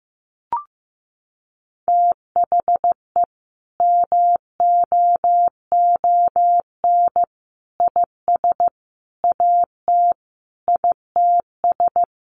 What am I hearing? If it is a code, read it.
Morse code, heard as a keyed tone, about 15 wpm, 700 Hz. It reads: THE MOON IS AT ITS
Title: Sentences from Top 1300 Words 15wpm